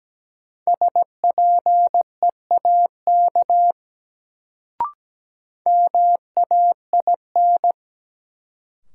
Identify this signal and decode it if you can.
Morse code, heard as a keyed tone, about 17 wpm, 700 Hz. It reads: SPEAK MAIN
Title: Sentences from Top 1000 Words - Review Part 2 - New Word Drill - 17wpm